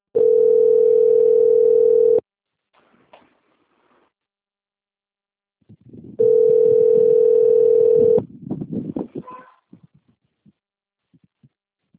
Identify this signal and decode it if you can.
Ringback tone (North America), 440 + 480 Hz.